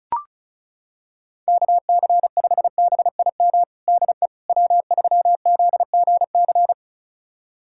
Morse code, heard as a keyed tone, about 36 wpm, 690 Hz. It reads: KC5BIM DE W3ZGC
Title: USA Callsign DE Callsign 35wpm